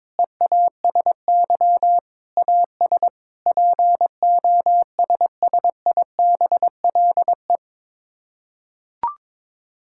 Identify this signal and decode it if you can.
Morse code, heard as a keyed tone, about 22 wpm, 700 Hz. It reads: EASY AS POSSIBLE
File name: practice / 22wpm / Sets of 7 Words - 22wpm